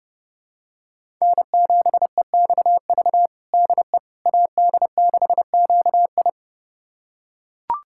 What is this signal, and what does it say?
Morse code, heard as a keyed tone, about 30 wpm, 700 Hz. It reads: N7EXV DE AD6QI
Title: USA Callsign DE Callsign - 5 Characters - 30wpm